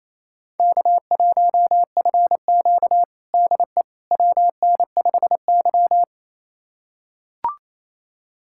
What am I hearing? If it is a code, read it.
Morse code, heard as a keyed tone, about 28 wpm, 700 Hz. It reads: K1FQ DE WN5Y
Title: USA Callsign DE Callsign - 4 Characters - 28wpm